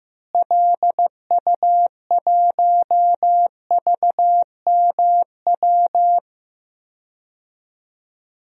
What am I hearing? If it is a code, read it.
Morse code, heard as a keyed tone, about 15 wpm, 700 Hz. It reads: LU1VMW
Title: DXCC-100 Call Signs - 6 Characters - Repeated 3x - 15wpm